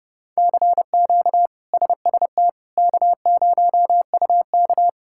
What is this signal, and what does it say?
Morse code, heard as a keyed tone, about 30 wpm, 700 Hz. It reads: CQ SST K0UK
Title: K1USN SST QSO Practice - 30wpm